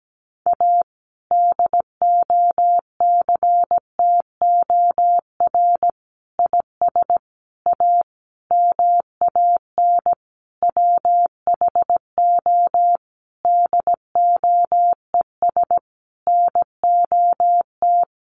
Morse code, heard as a keyed tone, about 17 wpm, 700 Hz. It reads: A DOCTOR IS A MAN WHO DOES NOT
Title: Sentences from Top 1500 Words 17wpm